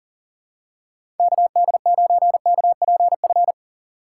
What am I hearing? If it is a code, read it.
Morse code, heard as a keyed tone, about 37 wpm, 700 Hz. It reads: KD9KPF
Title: USA Call Signs - 2x Character Spacing - 40wpm